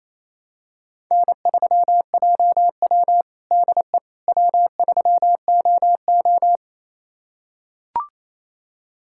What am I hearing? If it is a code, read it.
Morse code, heard as a keyed tone, about 28 wpm, 700 Hz. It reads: N3JW DE W3OO
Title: USA Callsign DE Callsign - 4 Characters - 28wpm